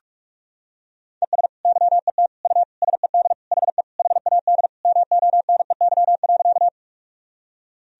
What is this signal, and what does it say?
Morse code, heard as a keyed tone, about 41 wpm, 700 Hz. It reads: ES YET U SED HE HAD MONEY.
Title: Sentences from Top 500 Words and Abbreviations 45wpm